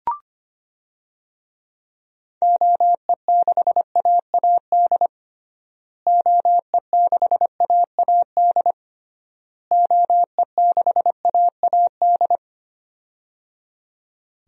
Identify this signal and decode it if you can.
Morse code, heard as a keyed tone, about 25 wpm, 700 Hz. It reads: OE6AAD OE6AAD OE6AAD
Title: DXCC-100 Call Signs - 6 Characters - Repeated 3x - 25wpm